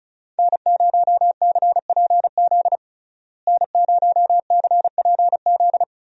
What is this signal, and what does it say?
Morse code, heard as a keyed tone, about 36 wpm, 700 Hz. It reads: N0CPZ N0CPZ
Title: USA Call Signs - 5 Characters - Repeated 3x - 35wpm